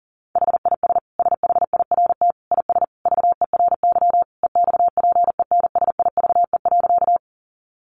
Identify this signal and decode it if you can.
Morse code, heard as a keyed tone, about 40 wpm, 700 Hz. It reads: HIS SHIRT IS VERY EXPENSIVE.
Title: Additional Easy Phrases 40wpm